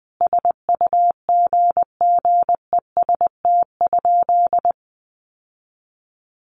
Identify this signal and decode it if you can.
Morse code, heard as a keyed tone, about 20 wpm, 700 Hz. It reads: SUGGEST?
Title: Sentences from Top 1300 Words 20wpm